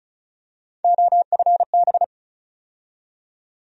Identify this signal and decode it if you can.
Morse code, heard as a keyed tone, about 36 wpm, 700 Hz. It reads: OFB
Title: USA Call Sign Postfixes - 3 Letters 35wpm